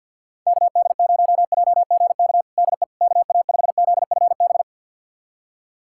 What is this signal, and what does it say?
Morse code, heard as a keyed tone, about 51 wpm, 700 Hz. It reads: KD0JGK DE KA5ZFB
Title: USA Callsign DE Callsign - 6 Characters - 50wpm